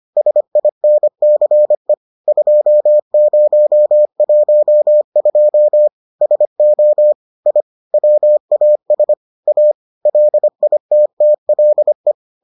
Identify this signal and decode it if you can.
Morse code, heard as a keyed tone, about 25 wpm, 600 Hz. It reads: SINCE 2012 SO I WAS A LITTLE